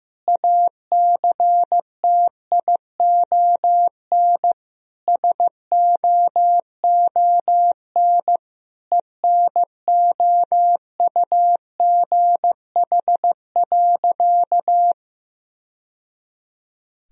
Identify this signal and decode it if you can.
Morse code, heard as a keyed tone, about 15 wpm, 700 Hz. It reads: ACTION SOON ENOUGH.
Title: Sentences from Top 1000 Words - Review Part 4 - 15wpm